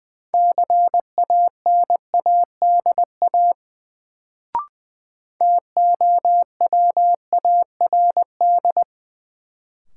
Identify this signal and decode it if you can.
Morse code, heard as a keyed tone, about 20 wpm, 700 Hz. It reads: CANADA TOWARD
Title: Sets of 5 Words - New Word Drill - 20wpm